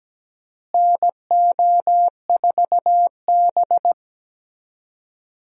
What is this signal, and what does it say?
Morse code, heard as a keyed tone, about 17 wpm, 700 Hz. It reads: NO4B
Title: USA Call Signs - 4 Characters - Repeated 2x - 17wpm